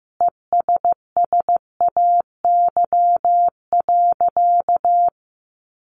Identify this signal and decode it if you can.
Morse code, heard as a keyed tone, about 15 wpm, 700 Hz. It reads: ESSAY.
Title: Sentences from Top 1500 Words 15wpm